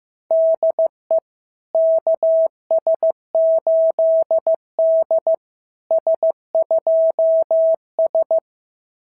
Morse code, heard as a keyed tone, about 15 wpm, 650 Hz. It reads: DE KS8D S2S
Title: SOTA QSO Practice 15wpm